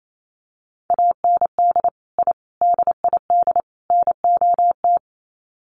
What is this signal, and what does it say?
Morse code, heard as a keyed tone, about 28 wpm, 700 Hz. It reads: AND I DID NOT
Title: Sets of 4 Words 28wpm